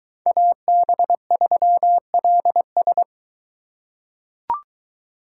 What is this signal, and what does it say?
Morse code, heard as a keyed tone, about 23 wpm, 700 Hz. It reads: AB3LS